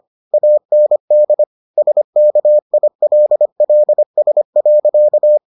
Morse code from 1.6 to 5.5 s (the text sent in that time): SKILLS.